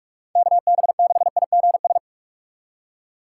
Morse code, heard as a keyed tone, about 46 wpm, 700 Hz. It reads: KB6IGS
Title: USA Call Signs - 6 Characters - 3x Character Spacing - 45wpm